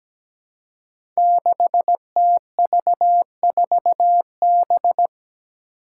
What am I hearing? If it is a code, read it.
Morse code, heard as a keyed tone, about 17 wpm, 700 Hz. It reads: 6TV4B
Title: Five Challenging Letter-Number Combinations 17wpm